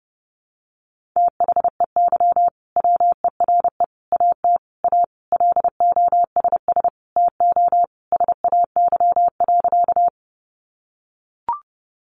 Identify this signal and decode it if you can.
Morse code, heard as a keyed tone, about 30 wpm, 700 Hz. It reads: THEY WERE AT A LOSS TO SAY.
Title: Sentences from Top 1400 Words 30wpm